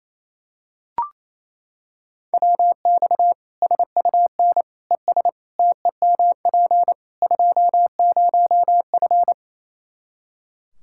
Morse code, heard as a keyed tone, about 28 wpm, 700 Hz. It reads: WX SUN ES TEMP 20F
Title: Practice QSO - Exchange 2 - Info - 28wpm